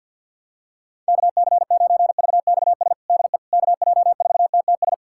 Morse code, heard as a keyed tone, about 50 wpm, 700 Hz. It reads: KC9VXS DE KJ4TTS
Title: USA Callsign DE Callsign - 6 Characters - 50wpm